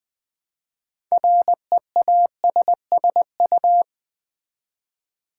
Morse code, heard as a keyed tone, about 20 wpm, 700 Hz. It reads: REASSU
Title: Build A Word - 2 Letters At A Time - 8-Letter Words - 20wpm